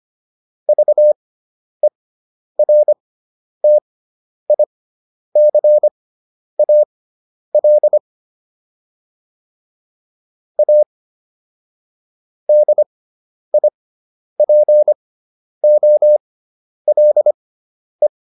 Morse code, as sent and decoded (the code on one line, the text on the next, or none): ...- . .-. - .. -.-. .- .-.. / .- / -.. .. .--. --- .-.. .
VERTICAL A DIPOLE